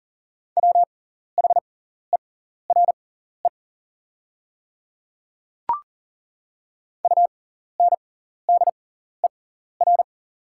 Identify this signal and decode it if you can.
Morse code, heard as a keyed tone, about 40 wpm, spaced as for 14 wpm, 690 Hz. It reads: WHERE UNDER
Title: 6x Character Spacing - Top 500 Words - 40wpm